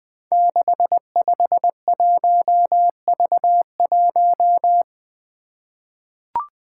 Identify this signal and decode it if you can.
Morse code, heard as a keyed tone, about 21 wpm, 700 Hz. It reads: 651V1
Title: Five Challenging Letter-Number Combinations 20wpm